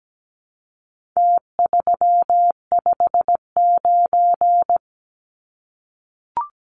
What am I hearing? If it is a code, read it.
Morse code, heard as a keyed tone, about 17 wpm, 700 Hz. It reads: T359
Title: Numbers - Contest Style - 4-Digit 17wpm